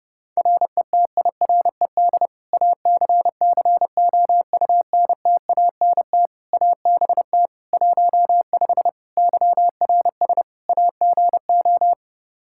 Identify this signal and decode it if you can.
Morse code, heard as a keyed tone, about 30 wpm, 700 Hz. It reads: RETIRED ACCOUNTANT ABT 15 YRS AGO
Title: Practice QSO - Exchange 3 - More Info - 30wpm